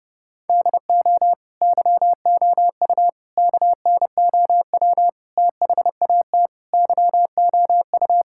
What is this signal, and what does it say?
Morse code, heard as a keyed tone, about 30 wpm, 700 Hz. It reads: DO YOU KNOW THAT YOU
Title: Sentences from Top 1000 Words - Review Part 5 - 30wpm